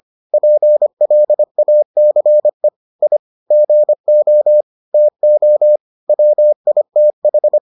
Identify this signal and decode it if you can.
Morse code, heard as a keyed tone, about 25 wpm, 600 Hz. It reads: PLACE I GO TO WITH